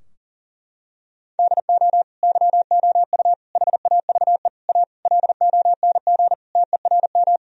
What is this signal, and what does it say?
Morse code, heard as a keyed tone, about 40 wpm, 700 Hz. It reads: DO YOU HAVE A LONG TERM
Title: Sentences from Top 1000 Words - Review Part 4 - 40wpm